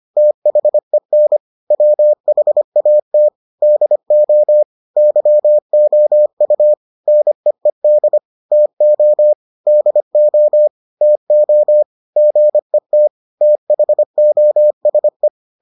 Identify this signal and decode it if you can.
Morse code, heard as a keyed tone, about 25 wpm, 600 Hz. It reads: THEN WHAT DO YOU NEED TO DO TO GET THOSE